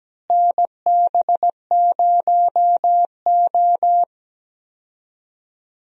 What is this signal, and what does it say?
Morse code, heard as a keyed tone, about 17 wpm, 700 Hz. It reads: NB0O
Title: USA Call Signs - 4 Characters - Repeated 2x - 17wpm